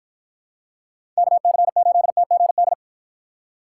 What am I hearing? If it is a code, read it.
Morse code, heard as a keyed tone, about 48 wpm, 700 Hz. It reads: KC8TGD